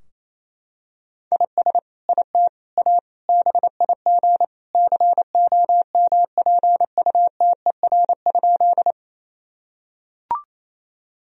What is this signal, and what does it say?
Morse code, heard as a keyed tone, about 28 wpm, 700 Hz. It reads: IS IT A BIG COMPUTER?
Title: Sentences from Top 1000 Words - Review Part 3 - 28wpm